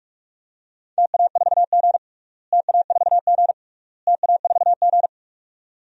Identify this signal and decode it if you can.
Morse code, heard as a keyed tone, about 45 wpm, 700 Hz. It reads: TA4G TA4G TA4G
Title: DXCC-100 Call Signs - 4 Characters - Repeated 3x - 45wpm